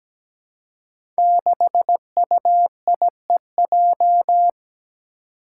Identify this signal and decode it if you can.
Morse code, heard as a keyed tone, about 17 wpm, 700 Hz. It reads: 6UIEJ
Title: Five Challenging Letter-Number Combinations 17wpm